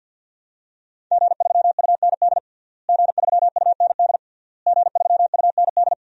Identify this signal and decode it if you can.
Morse code, heard as a keyed tone, about 52 wpm, 700 Hz. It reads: G3UND G3UND G3UND
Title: DXCC-100 Call Signs - 5 Characters - Repeated 3x - 50wpm